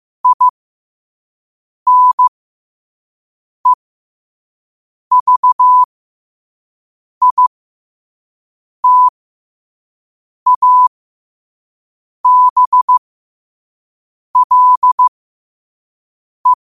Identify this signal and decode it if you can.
Morse code, heard as a keyed tone, about 15 wpm, spaced as for 5 wpm, 1000 Hz. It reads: INEVITABLE